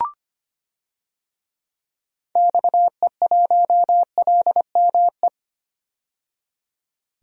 Morse code, heard as a keyed tone, about 25 wpm, 700 Hz. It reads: XE1LME
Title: DXCC-100 Call Signs - 6 Characters - 25wpm